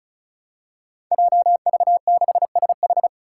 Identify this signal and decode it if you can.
Morse code, heard as a keyed tone, about 36 wpm, 700 Hz. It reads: JV6SH
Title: Five Challenging Letter-Number Combinations 35wpm